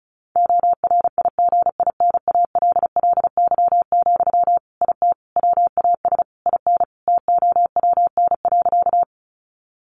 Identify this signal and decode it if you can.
Morse code, heard as a keyed tone, about 34 wpm, 700 Hz. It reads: ORIGINALLY, IT WAS IN TOWN.
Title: Sentences from Top 1500 Words 35wpm